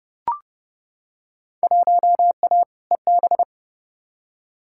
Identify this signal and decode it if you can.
Morse code, heard as a keyed tone, about 30 wpm, 700 Hz. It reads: 1A EB
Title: Field Day Exchanges - 30wpm